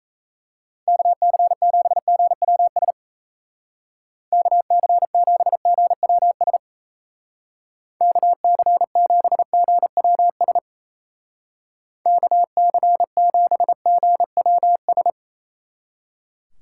Morse code, heard as a keyed tone, about 34 wpm, 700 Hz. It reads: KC7GWS KC7GWS KC7GWS KC7GWS
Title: Speed Racing - USA 2x3 Call Signs 28wpm